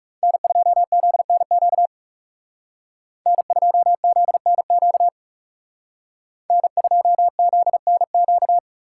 Morse code, heard as a keyed tone, about 37 wpm, 700 Hz. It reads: N2ZNQ N2ZNQ N2ZNQ
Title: Speed Racing - USA Call Signs 30wpm